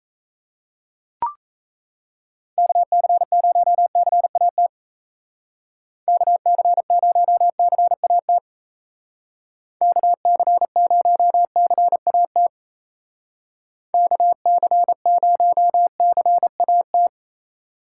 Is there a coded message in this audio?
Yes: Morse code from 2.6 to 17.1 s, heard as a keyed tone, about 36 wpm, 700 Hz. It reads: KC0CAT KC0CAT KC0CAT KC0CAT